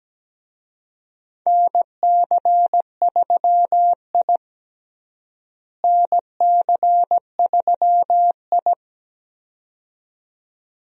Morse code, heard as a keyed tone, about 17 wpm, 700 Hz. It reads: NC3I NC3I
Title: USA Call Signs - 4 Characters - Repeated 2x - 17wpm